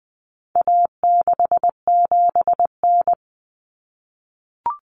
Morse code, heard as a keyed tone, about 20 wpm, 700 Hz. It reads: A67N
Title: Common Cut Numbers - 4-Digit 20wpm